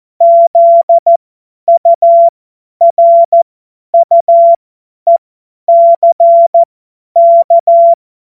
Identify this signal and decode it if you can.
Morse code, heard as a keyed tone, about 14 wpm, spaced as for 10 wpm, 680 Hz. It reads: ZURUECK
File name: Morsekurs / Witze / Witz 11. Januar 2022, 50 F70